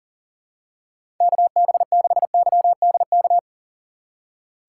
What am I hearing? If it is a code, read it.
Morse code, heard as a keyed tone, about 40 wpm, 700 Hz. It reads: KB6YDK
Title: USA Call Signs - 6 Characters - 4x Character Spacing - 40wpm